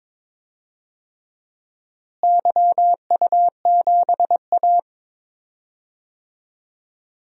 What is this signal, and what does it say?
Morse code, heard as a keyed tone, about 22 wpm, 700 Hz. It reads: YU7A
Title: DXCC-100 Call Signs - 4 Characters - Repeated 3x - 22wpm